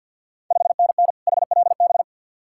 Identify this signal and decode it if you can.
Morse code, heard as a keyed tone, about 50 wpm, 690 Hz. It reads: 5NN HLB
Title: Ontario QSO Party Exchange 50wpm